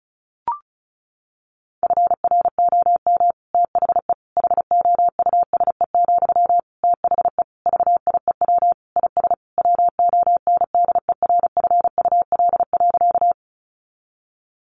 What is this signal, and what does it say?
Morse code, heard as a keyed tone, about 34 wpm, 700 Hz. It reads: FROM THE HOUSE, THE VIEW IS WONDERFUL.
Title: Sentences from Top 1100 Words 35wpm